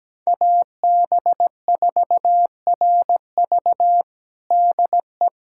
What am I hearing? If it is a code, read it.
Morse code, heard as a keyed tone, about 17 wpm, 700 Hz. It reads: AB4RV DE
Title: USA Callsign DE Callsign - 5 Characters - 17wpm